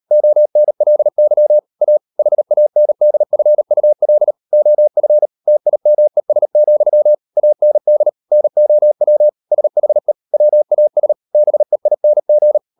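Morse code, heard as a keyed tone, about 38 wpm, 600 Hz. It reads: ONLY A HANDFUL OF TIMES, AND NOW SHE WAS BEING
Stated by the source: Morse code practice at 38 WPM, sending Ken Follett's novel Never in English, from CW med Gnister.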